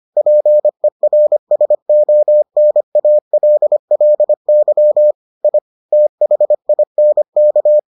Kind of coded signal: Morse code

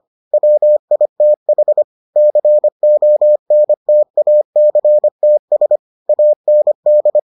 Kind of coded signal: Morse code

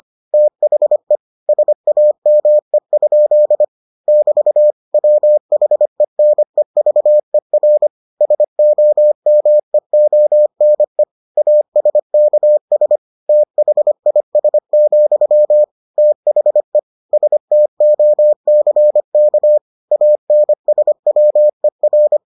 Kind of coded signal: Morse code

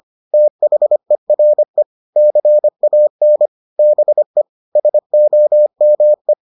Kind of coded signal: Morse code

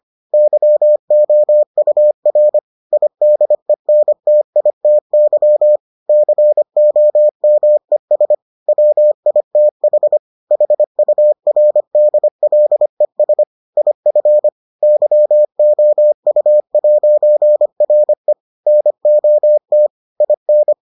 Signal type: Morse code

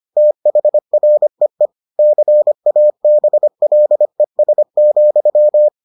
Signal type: Morse code

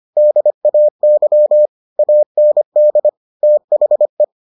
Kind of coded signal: Morse code

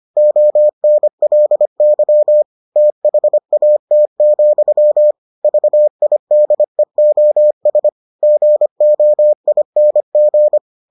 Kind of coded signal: Morse code

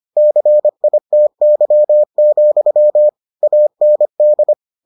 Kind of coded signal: Morse code

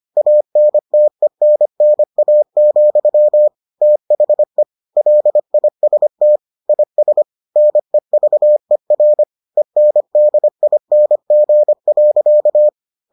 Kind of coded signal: Morse code

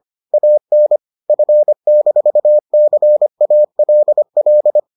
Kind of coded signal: Morse code

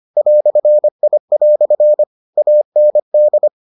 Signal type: Morse code

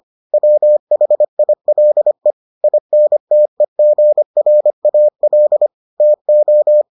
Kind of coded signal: Morse code

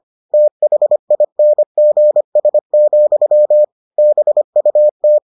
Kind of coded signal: Morse code

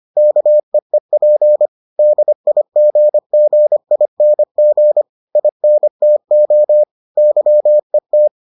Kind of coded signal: Morse code

